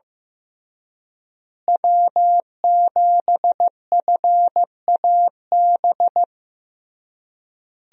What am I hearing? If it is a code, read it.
Morse code, heard as a keyed tone, about 16 wpm, 700 Hz. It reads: W7FAB
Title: USA Call Signs - 5 Characters - Repeated 3x - 15wpm